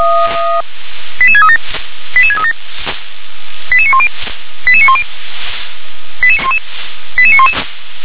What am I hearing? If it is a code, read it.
Five-tone selective call (ZVEI), heard as quick runs of five tones, tones of about 70 ms. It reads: 88637 88637 88910 88910 88911
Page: SDR/ZVEI